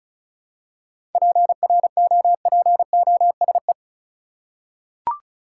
Morse code, heard as a keyed tone, about 36 wpm, 700 Hz. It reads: PROPOSE
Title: Seven Letter Words - 35wpm